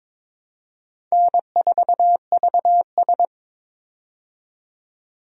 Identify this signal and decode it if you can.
Morse code, heard as a keyed tone, about 22 wpm, 700 Hz. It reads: N4VS
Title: USA Call Signs - 4 Characters - 22wpm